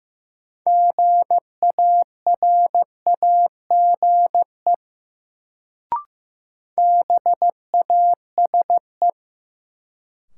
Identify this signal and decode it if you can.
Morse code, heard as a keyed tone, about 15 wpm, 700 Hz. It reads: GARAGE BASE
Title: Sentences from Top 1600 Words - New Word Drill Alt - 15wpm